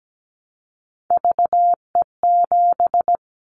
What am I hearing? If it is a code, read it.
Morse code, heard as a keyed tone, about 17 wpm, 700 Hz. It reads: VE7
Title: Callsign Prefixes for Top 100 DXCC - 17wpm